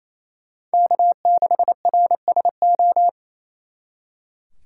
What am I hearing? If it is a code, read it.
Morse code, heard as a keyed tone, about 36 wpm, 700 Hz. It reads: K6RSO